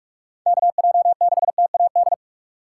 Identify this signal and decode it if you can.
Morse code, heard as a keyed tone, about 44 wpm, 700 Hz. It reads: KJ6TAD
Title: USA Call Signs - 6 Characters - 6x Character Spacing - 45wpm